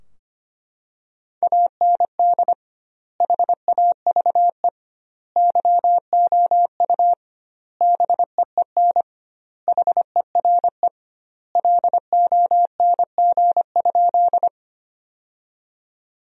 Morse code, heard as a keyed tone, about 25 wpm, 700 Hz. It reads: AND HAVE YOU BEEN HERE LONG?
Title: 2x Word-Spacing - Sentences from Top 200 Words - 25wpm